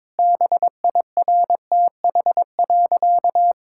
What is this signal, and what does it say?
Morse code, heard as a keyed tone, about 22 wpm, 700 Hz. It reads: BIRTH.
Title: Sentences from Top 1000 Words - Review Part 6 - 22wpm